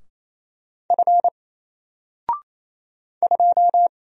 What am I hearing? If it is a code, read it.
Morse code, heard as a keyed tone, about 28 wpm, 700 Hz. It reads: F 2
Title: Single Character (Letter:Number:Punctuation:Prosign) Alt 28wpm